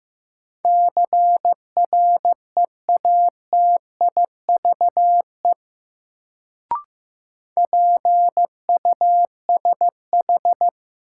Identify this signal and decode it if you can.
Morse code, heard as a keyed tone, about 16 wpm, 700 Hz. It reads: CREATIVE PUSH
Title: Sentences from Top 1300 Words - New Word Drill Alt - 15wpm